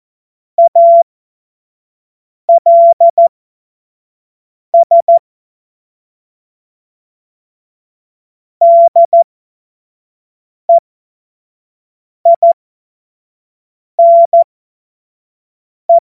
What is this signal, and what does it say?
Morse code, heard as a keyed tone, about 14 wpm, spaced as for 5 wpm, 680 Hz. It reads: ALS DEINE
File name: Morsekurs / Witze / Witz 18. Mai 2021, 25 F70